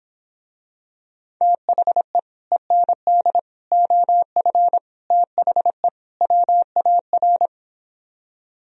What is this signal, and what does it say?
Morse code, heard as a keyed tone, about 26 wpm, 700 Hz. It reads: THE END OF THE WAR